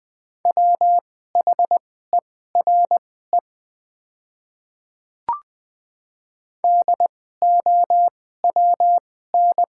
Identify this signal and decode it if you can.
Morse code, heard as a keyed tone, about 20 wpm, spaced as for 14 wpm, 700 Hz. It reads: WHERE DOWN
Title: 2x Character Spacing - Top 200 Words - 20wpm